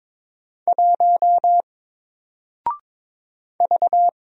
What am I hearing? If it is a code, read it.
Morse code, heard as a keyed tone, about 22 wpm, 700 Hz. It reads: 1 V
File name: Single-Letter-Number-22wpm